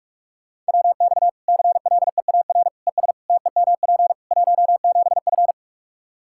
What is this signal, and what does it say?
Morse code, heard as a keyed tone, about 42 wpm, 700 Hz. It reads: WX CLEAR ES TEMP 17F
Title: Practice QSO - Exchange 2 - Info - 45wpm